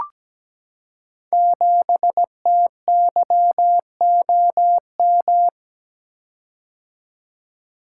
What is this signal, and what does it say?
Morse code, heard as a keyed tone, about 17 wpm, 700 Hz. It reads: 7TYOM
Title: Code Groups - 5 Letter:Numbers - 17wpm